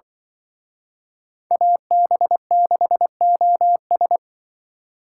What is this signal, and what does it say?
Morse code, heard as a keyed tone, about 23 wpm, 700 Hz. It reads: AB6OS
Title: Speed Racing - USA Call Signs 20wpm